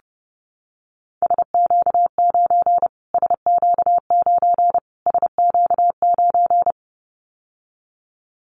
Morse code, heard as a keyed tone, about 30 wpm, 700 Hz. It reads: SQ9 SQ9 SQ9
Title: Callsign Prefixes for Top 25 DXCC - Repeated 3x - 30wpm